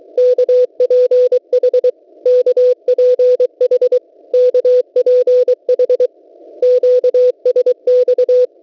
Morse code, heard as a keyed tone, about 23 wpm, 500 Hz. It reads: KPH KPH KPH QSX
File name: WebSDR records / w7pua-2.ddns.net 8642.00 cw